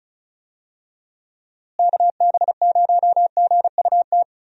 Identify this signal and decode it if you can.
Morse code, heard as a keyed tone, about 36 wpm, 700 Hz. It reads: KB0GUT
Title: USA Call Signs - 6 Characters - Repeated 3x - 35wpm